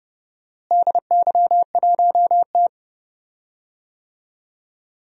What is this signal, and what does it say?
Morse code, heard as a keyed tone, about 31 wpm, 700 Hz. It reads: DY1T
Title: DXCC-100 Call Signs - 4 Characters - Repeated 3x - 30wpm